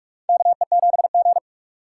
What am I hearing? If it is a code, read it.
Morse code, heard as a keyed tone, about 46 wpm, 700 Hz. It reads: KE7G
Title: USA Call Signs - 4 Characters - 6x Character Spacing - 45wpm